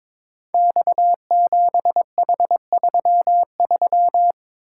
Morse code, heard as a keyed tone, about 22 wpm, 700 Hz. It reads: X7H33
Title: Code Groups - 5 Letter:Numbers - 22wpm